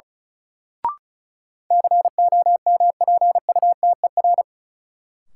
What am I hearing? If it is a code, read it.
Morse code, heard as a keyed tone, about 36 wpm, 700 Hz. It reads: COMPUTER
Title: Sentences from Top 1000 Words - Review Part 3 - New Word Drill - 35wpm